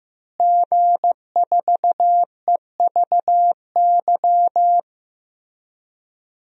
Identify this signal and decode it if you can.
Morse code, heard as a keyed tone, about 15 wpm, 700 Hz. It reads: G4EVY
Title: DXCC-100 Call Signs - 5 Characters - Repeated 3x - 15wpm